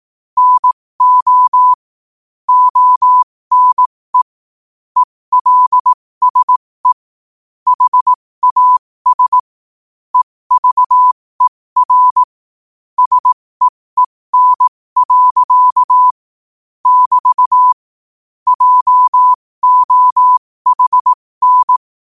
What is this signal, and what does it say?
Morse code, heard as a keyed tone, about 18 wpm, spaced as for 15 wpm, 1000 Hz. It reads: NO ONE ELSE HAS EVER SEEN. <BT> JOHN
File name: qotdfiles / 15wpm